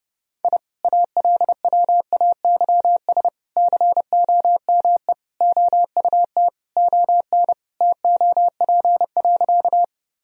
Morse code, heard as a keyed tone, about 30 wpm, 700 Hz. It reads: I ALWAYS COME OUT ON TOP.
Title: Sentences from Top 1000 Words - Review Part 2 - 30wpm